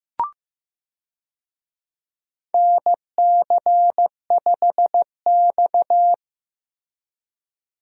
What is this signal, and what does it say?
Morse code, heard as a keyed tone, about 15 wpm, 700 Hz. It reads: NC5X
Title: USA Call Signs - 4 Characters - Repeated 3x - 15wpm